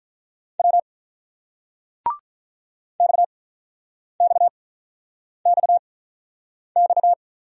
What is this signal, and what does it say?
Morse code, heard as a keyed tone, about 46 wpm, spaced as for 9 wpm, 690 Hz. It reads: W XXXX